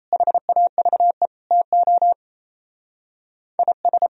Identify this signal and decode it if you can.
Morse code, heard as a keyed tone, about 36 wpm, 700 Hz. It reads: HAVE TO IS